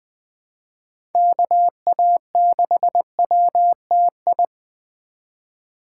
Morse code, heard as a keyed tone, about 20 wpm, 700 Hz. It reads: KA6WTI
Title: USA Call Signs - 6 Characters - 2x Character Spacing - 20wpm